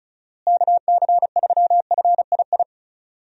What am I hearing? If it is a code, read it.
Morse code, heard as a keyed tone, about 34 wpm, 700 Hz. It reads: KC3FII
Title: USA Call Signs - 6 Characters - Repeated 2x - 35wpm